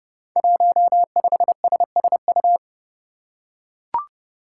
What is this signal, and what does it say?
Morse code, heard as a keyed tone, about 30 wpm, 700 Hz. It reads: 15SSU
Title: Five Challenging Letter-Number Combinations 30wpm